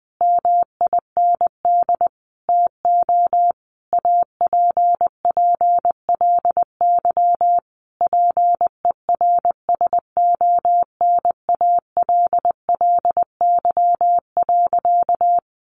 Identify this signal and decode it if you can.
Morse code, heard as a keyed tone, about 20 wpm, 700 Hz. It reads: MIND TO APPLY PERSONALLY.
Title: Sentences from Top 1400 Words 20wpm